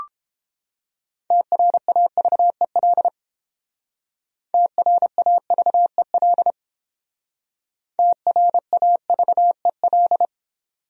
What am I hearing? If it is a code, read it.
Morse code, heard as a keyed tone, about 30 wpm, 700 Hz. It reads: TRAVEL TRAVEL TRAVEL